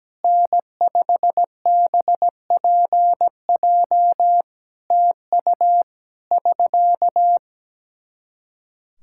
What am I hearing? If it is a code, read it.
Morse code, heard as a keyed tone, about 17 wpm, 700 Hz. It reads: N5BPJ TU <SK>
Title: Practice QSO - Exchange 4 - Ending - 17wpm